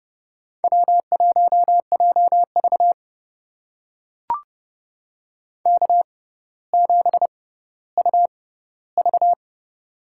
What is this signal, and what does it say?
Morse code, heard as a keyed tone, about 30 wpm, spaced as for 15 wpm, 700 Hz. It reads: W1JV K7UV
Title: USA Call Signs - 4 Characters - 6x Character Spacing - 30wpm